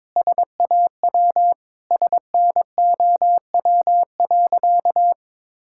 Morse code, heard as a keyed tone, about 22 wpm, 700 Hz. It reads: SAW SNOW.
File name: practice / 22wpm / Sentences with 3-Word Phrases - 22wpm